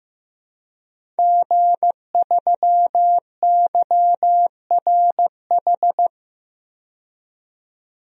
Morse code, heard as a keyed tone, about 15 wpm, 700 Hz. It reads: G3YRH
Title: DXCC-100 Call Signs - 5 Characters - 15wpm